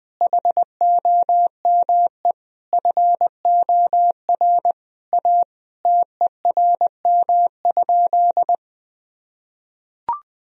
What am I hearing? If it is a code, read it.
Morse code, heard as a keyed tone, about 20 wpm, 700 Hz. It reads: HOME FOR A TERM?
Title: Sentences from Top 1000 Words - Review Part 4 - 20wpm